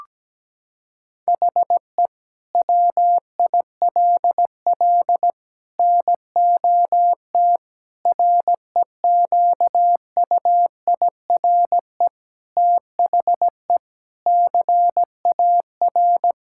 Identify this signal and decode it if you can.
Morse code, heard as a keyed tone, about 17 wpm, 700 Hz. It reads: HE WILL NOT REQUIRE THE CAR
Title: Sentences from Top 1000 Words - Review Part 6 - 17wpm